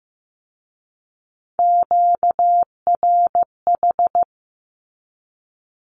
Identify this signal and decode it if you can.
Morse code, heard as a keyed tone, about 15 wpm, 700 Hz. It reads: QRH
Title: Q-Codes with Spoken Meaning - 15wpm